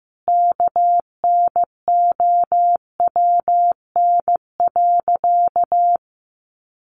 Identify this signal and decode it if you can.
Morse code, heard as a keyed tone, about 15 wpm, 700 Hz. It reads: KNOWN.
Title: Sentences from Top 1100 Words 15wpm